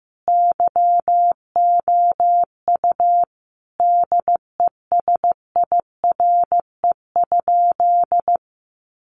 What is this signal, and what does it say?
Morse code, heard as a keyed tone, about 15 wpm, 700 Hz. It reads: YOU DESIRE?
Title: Sentences from Top 1100 Words 15wpm